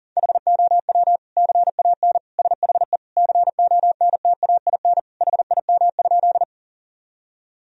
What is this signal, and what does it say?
Morse code, heard as a keyed tone, about 37 wpm, 700 Hz. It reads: HOW CAN SHE CONTAIN HIM?
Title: Sentences from Top 1000 Words - Review Part 2 - 40wpm